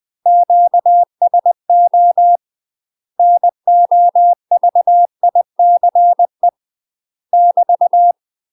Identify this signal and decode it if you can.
Morse code, heard as a keyed tone, about 20 wpm, 700 Hz. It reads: QSO NOVICE <BT>
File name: UK MASTER QSOs TRAINING / 20WPM / UK MASTER QSOs 0017